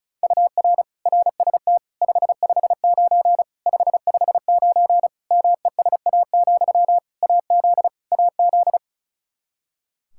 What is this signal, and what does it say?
Morse code, heard as a keyed tone, about 36 wpm, 700 Hz. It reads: UR RST 559 559 MESA, AZ AZ
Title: Practice QSO - Exchange 1 - Beginning - 35wpm